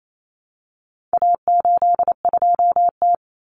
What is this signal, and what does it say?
Morse code, heard as a keyed tone, about 28 wpm, 700 Hz. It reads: A82T
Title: Common Cut Numbers - 4-Digit 28wpm